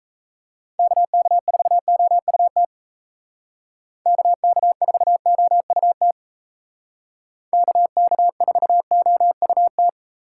Morse code, heard as a keyed tone, about 33 wpm, 700 Hz. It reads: KK4OUT KK4OUT KK4OUT
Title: Speed Racing - USA 2x3 Call Signs 28wpm